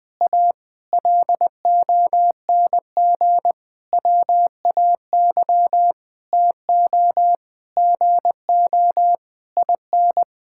Morse code, heard as a keyed tone, about 20 wpm, 700 Hz. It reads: A LONG WAY TO GO IN
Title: Sentences with 7-Word Phrases - 20wpm